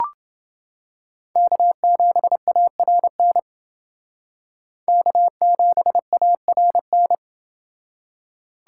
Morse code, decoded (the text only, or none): K7ARN K7ARN